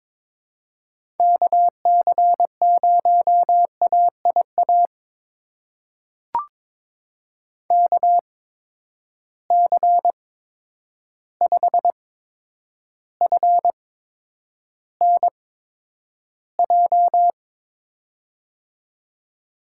Morse code, heard as a keyed tone, about 22 wpm, spaced as for 9 wpm, 700 Hz. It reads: KC0AIA KC5FNJ
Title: USA Call Signs - 6 Characters - 8x Character Spacing - 22wpm